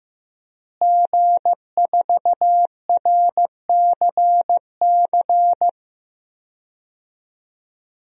Morse code, heard as a keyed tone, about 14 wpm, 700 Hz. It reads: G4RCC
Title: DXCC-100 Call Signs - 5 Characters - Repeated 2x - 15wpm